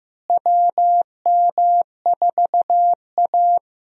Morse code, heard as a keyed tone, about 15 wpm, 700 Hz. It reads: WM4A
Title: USA Call Signs - 4 Characters - Repeated 3x - 15wpm